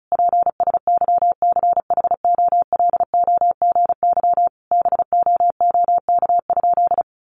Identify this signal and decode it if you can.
Morse code, heard as a keyed tone, about 34 wpm, 700 Hz. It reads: PSYCHOLOGY BOOK?
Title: Sentences from Top 1400 Words 35wpm